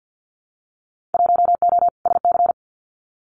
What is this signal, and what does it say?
Morse code, heard as a keyed tone, about 50 wpm, 700 Hz. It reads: 1O SC
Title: Winter Field Day Exchanges 50wpm